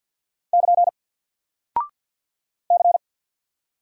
Morse code, heard as a keyed tone, about 48 wpm, 700 Hz. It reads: <KN> C
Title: Single Character (Letter:Number:Punctuation:Prosign) 50wpm